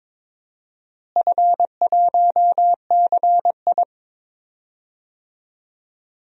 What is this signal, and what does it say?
Morse code, heard as a keyed tone, about 22 wpm, 700 Hz. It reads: F1CI
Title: DXCC-100 Call Signs - 4 Characters - Repeated 3x - 22wpm